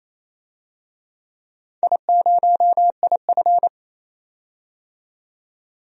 Morse code, heard as a keyed tone, about 28 wpm, 700 Hz. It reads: I0IF
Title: DXCC-100 Call Signs - 4 Characters - 28wpm